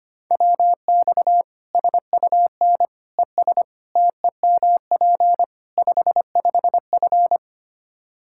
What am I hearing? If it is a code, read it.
Morse code, heard as a keyed tone, about 25 wpm, 700 Hz. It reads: WX SUN ES TEMP 55F
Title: Practice QSO - Exchange 2 - Info - 25wpm